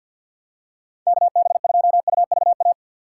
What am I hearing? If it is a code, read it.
Morse code, heard as a keyed tone, about 45 wpm, 690 Hz. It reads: KB2UVA
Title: USA Call Signs - 6 Characters - 3x Character Spacing - 50wpm